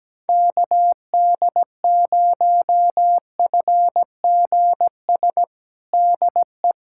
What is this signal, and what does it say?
Morse code, heard as a keyed tone, about 17 wpm, 700 Hz. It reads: KD0FGS DE
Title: USA Callsign DE Callsign - 6 Characters - 17wpm